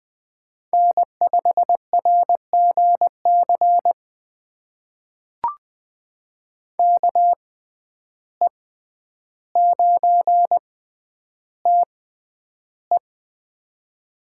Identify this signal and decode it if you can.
Morse code, heard as a keyed tone, about 20 wpm, spaced as for 10 wpm, 700 Hz. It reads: N5RGC KE9TE
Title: USA Call Signs - 5 Characters - 6x Character Spacing - 20wpm